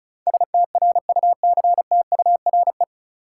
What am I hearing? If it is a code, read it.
Morse code, heard as a keyed tone, about 36 wpm, 700 Hz. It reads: STRUCTURE
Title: Sentences from Top 1000 Words - Review Part 6 - New Word Drill - 35wpm